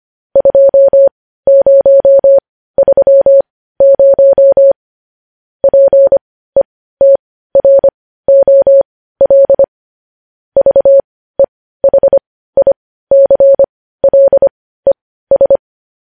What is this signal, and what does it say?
Morse code, heard as a keyed tone, about 25 wpm, spaced as for 15 wpm, 550 Hz. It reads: 2030 PETROL VEHICLES
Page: news-headlines-in-morse-code-at-15-wpm